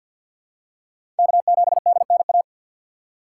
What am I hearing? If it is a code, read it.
Morse code, heard as a keyed tone, about 48 wpm, 700 Hz. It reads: K7DNA
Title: USA Call Signs - 5 Characters - 2x Character Spacing - 50wpm